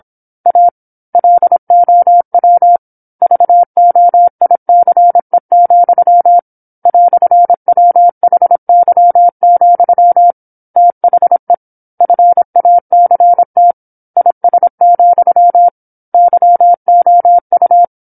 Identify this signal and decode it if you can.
Morse code, heard as a keyed tone, about 26 wpm, 700 Hz. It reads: A LOW VOICE, "WHY, THE FACT IS, YOU
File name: 08-Chapter-08 26 WPM